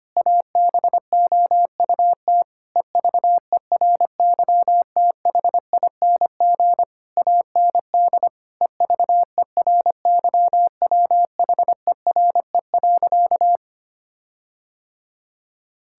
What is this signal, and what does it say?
Morse code, heard as a keyed tone, about 25 wpm, 700 Hz. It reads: ABOUT EVERYTHING AND EVERYWHERE.
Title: Sentences from Top 1600 Words 25wpm